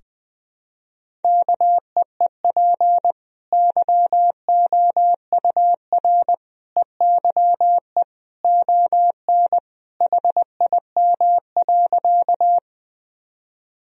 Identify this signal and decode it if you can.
Morse code, heard as a keyed tone, about 21 wpm, 700 Hz. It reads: KEEP YOUR EYE ON HIM.
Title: Sentences from Top 1000 Words - Review Part 1 - 20wpm